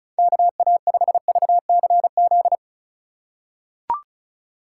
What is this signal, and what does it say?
Morse code, heard as a keyed tone, about 34 wpm, 700 Hz. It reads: KA5VCZ
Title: USA Call Signs - 4x Character Spacing - 35wpm